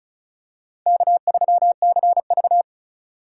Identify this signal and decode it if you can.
Morse code, heard as a keyed tone, about 34 wpm, 700 Hz. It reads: K3CV
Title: USA Call Signs - 4 Characters - 4x Character Spacing - 35wpm